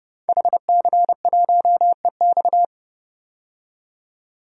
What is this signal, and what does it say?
Morse code, heard as a keyed tone, about 31 wpm, 700 Hz. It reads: HC1EX
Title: DXCC-100 Call Signs - 5 Characters - 30wpm